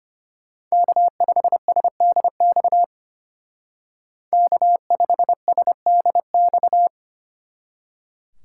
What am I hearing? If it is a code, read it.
Morse code, heard as a keyed tone, about 31 wpm, 700 Hz. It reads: K5SDX K5SDX